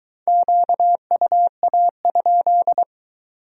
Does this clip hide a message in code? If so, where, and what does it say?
Yes: Morse code from 0.3 to 2.8 s, heard as a keyed tone, about 22 wpm, 700 Hz. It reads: QUA?